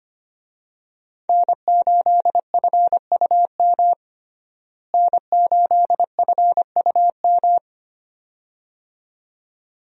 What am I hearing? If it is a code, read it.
Morse code, heard as a keyed tone, about 25 wpm, 700 Hz. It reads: N8FUM N8FUM
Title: USA Call Signs - 5 Characters - Repeated 2x - 25wpm